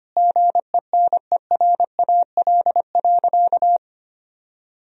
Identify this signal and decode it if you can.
Morse code, heard as a keyed tone, about 25 wpm, 700 Hz. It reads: GENERAL.
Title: Sentences from Top 1000 Words - Review Part 3 - 25wpm